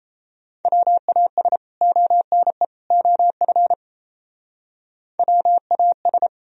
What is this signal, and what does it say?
Morse code, heard as a keyed tone, about 36 wpm, 700 Hz. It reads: WAS ONE OF WAS